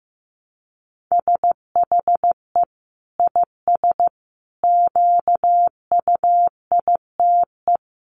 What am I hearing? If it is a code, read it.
Morse code, heard as a keyed tone, about 15 wpm, 700 Hz. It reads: SHE IS QUITE
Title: Sentences from Top 1500 Words 15wpm